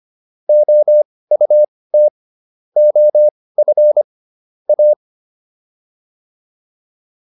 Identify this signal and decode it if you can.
Morse code, heard as a keyed tone, about 25 wpm, spaced as for 18 wpm, 600 Hz. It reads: OUT OF A